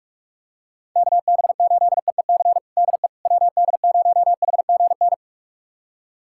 Morse code, heard as a keyed tone, about 46 wpm, 700 Hz. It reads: KB8EEC DE WD0HGN
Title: USA Callsign DE Callsign - 6 Characters - 45wpm